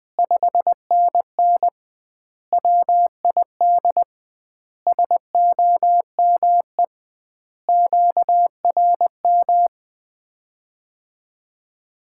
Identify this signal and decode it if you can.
Morse code, heard as a keyed tone, about 20 wpm, 700 Hz. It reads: 5NN WID SOME QRM
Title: Common QSO Phrases - 2x Word-Spacing - 20wpm